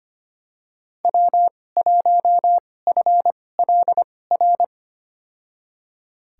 Morse code, heard as a keyed tone, about 25 wpm, spaced as for 18 wpm, 700 Hz. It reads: W1FLR
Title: USA Call Signs - 2x Character Spacing - 25wpm